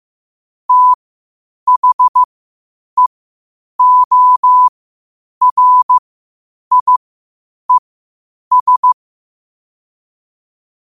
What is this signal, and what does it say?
Morse code, heard as a keyed tone, about 15 wpm, spaced as for 8 wpm, 1000 Hz. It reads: THEORIES
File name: qotdfiles / 8wpm